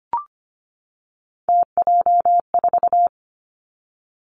Morse code, heard as a keyed tone, about 25 wpm, 700 Hz. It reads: TJ4
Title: Three Challenging Letter-Number Combinations 25wpm